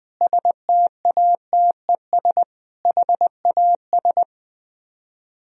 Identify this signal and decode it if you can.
Morse code, heard as a keyed tone, about 20 wpm, 700 Hz. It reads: STATES HAS
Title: Speed-Racing - Sets of 4 Words 17wpm